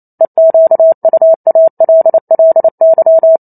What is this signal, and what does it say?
Morse code, heard as a keyed tone, about 28 wpm, 650 Hz. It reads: EQUALLY